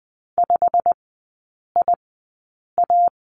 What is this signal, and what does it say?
Morse code, heard as a keyed tone, about 20 wpm, spaced as for 8 wpm, 700 Hz. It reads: 5IA
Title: 100 Lesson 17e - Letter u - Three Characters 20wpm